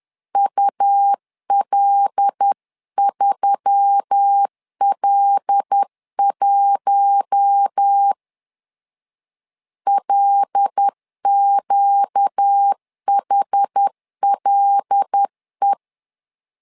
Morse code, heard as a keyed tone, about 10 wpm, 800 Hz. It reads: UL3L1 LQHLE
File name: LEZIONI / 5 LBVC / 5 L 10wpm 2259